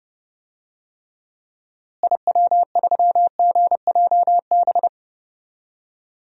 Morse code, heard as a keyed tone, about 30 wpm, 700 Hz. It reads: IW3GJB
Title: DXCC-100 Call Signs - 6 Characters - Repeated 2x - 30wpm